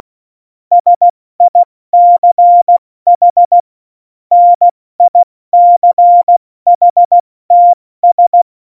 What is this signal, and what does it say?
Morse code, heard as a keyed tone, about 16 wpm, spaced as for 14 wpm, 700 Hz. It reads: SICH NICHTS
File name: Morsekurs / Witze / Witz 28 April 2020, 70 F80